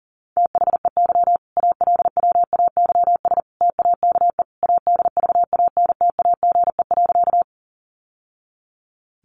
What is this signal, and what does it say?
Morse code, heard as a keyed tone, about 40 wpm, 700 Hz. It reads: THEY ALWAYS TAKE ADVANTAGE.
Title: Sentences from Top 900 Words 40wpm